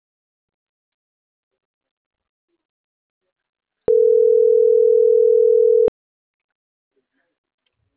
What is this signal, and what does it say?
Ringback tone (North America), 440 + 480 Hz.